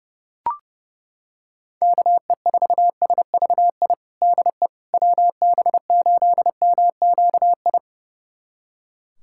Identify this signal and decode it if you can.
Morse code, heard as a keyed tone, about 30 wpm, 700 Hz. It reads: KE4SVI DE WB8MQI
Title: USA Callsign DE Callsign - 6 Characters - 30wpm